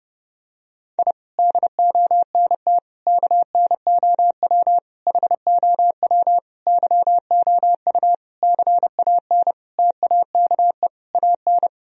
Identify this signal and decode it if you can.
Morse code, heard as a keyed tone, about 32 wpm, 700 Hz. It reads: I DONT KNOW HOW YOU CAN TAKE AN
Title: Sentences from Top 1000 Words - Review Part 2 - 30wpm